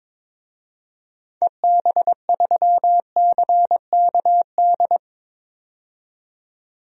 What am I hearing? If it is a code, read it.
Morse code, heard as a keyed tone, about 22 wpm, 700 Hz. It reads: EB3CKD
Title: DXCC-100 Call Signs - 6 Characters - 22wpm